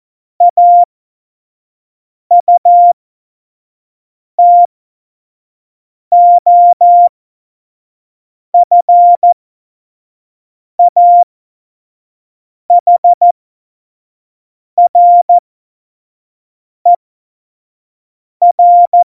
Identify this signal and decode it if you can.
Morse code, heard as a keyed tone, about 14 wpm, spaced as for 5 wpm, 700 Hz. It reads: AUTOFAHRER
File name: Morsekurs / Witze / Witz 12. Februar 2019, 25 F70